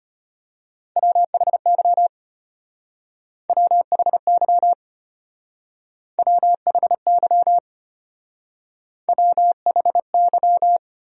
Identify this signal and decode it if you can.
Morse code, heard as a keyed tone, about 31 wpm, 700 Hz. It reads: WHY WHY WHY WHY
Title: Speed-Racing Top 200 Words 25wpm